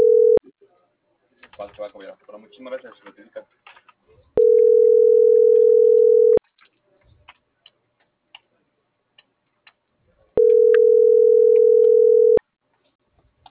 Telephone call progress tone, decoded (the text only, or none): ringback tone